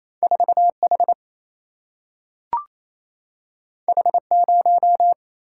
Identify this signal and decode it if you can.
Morse code, heard as a keyed tone, about 28 wpm, 700 Hz. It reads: 4H H0
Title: Single-Letter Single-Number 28wpm